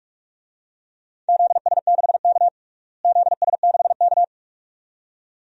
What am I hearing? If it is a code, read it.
Morse code, heard as a keyed tone, about 46 wpm, 700 Hz. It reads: ZS6K ZS6K
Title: DXCC-100 Call Signs - 4 Characters - Repeated 2x - 45wpm